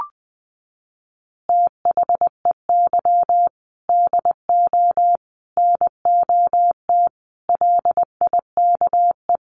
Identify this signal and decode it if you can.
Morse code, heard as a keyed tone, about 20 wpm, 700 Hz. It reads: THEY DO NOT LIKE
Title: Easy Phrases 20wpm